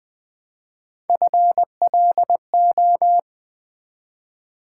Morse code, heard as a keyed tone, about 20 wpm, 700 Hz. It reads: FLO
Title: Build A Word - 1 Letter At A Time - 5-Letter Words - 20wpm